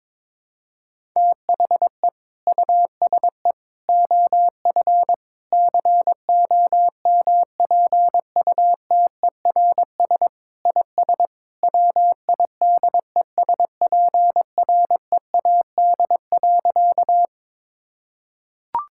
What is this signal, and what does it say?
Morse code, heard as a keyed tone, about 22 wpm, 700 Hz. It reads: THE USE OF COMPUTERS IS WIDESPREAD.
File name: practice / 22wpm / Sentences with 3-Word Phrases - 22wpm